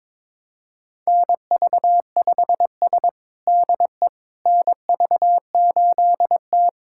Morse code, heard as a keyed tone, about 22 wpm, 700 Hz. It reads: NV5S DE NV8T
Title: USA Callsign DE Callsign - 4 Characters - 22wpm